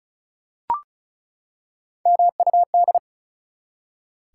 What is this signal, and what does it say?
Morse code, heard as a keyed tone, about 36 wpm, 700 Hz. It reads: MUD
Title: Three Letter Words - 35wpm